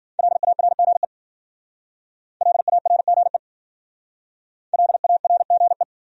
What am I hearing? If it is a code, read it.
Morse code, heard as a keyed tone, about 52 wpm, 700 Hz. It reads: LARGE LARGE LARGE